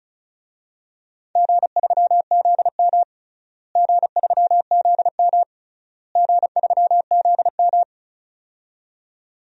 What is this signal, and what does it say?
Morse code, heard as a keyed tone, about 36 wpm, 700 Hz. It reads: G3ZM G3ZM G3ZM
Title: DXCC-100 Call Signs - 4 Characters - Repeated 3x - 35wpm